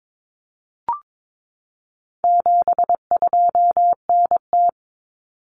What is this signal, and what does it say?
Morse code, heard as a keyed tone, about 22 wpm, 700 Hz. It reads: 72NT
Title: Common Cut Numbers - 4-Digit 22wpm